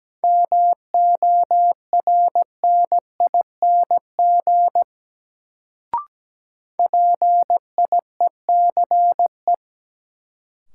Morse code, heard as a keyed tone, about 17 wpm, 700 Hz. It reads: MORNING PIECE
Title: Sentences from Top 400 Words - New Word Drill - 17wpm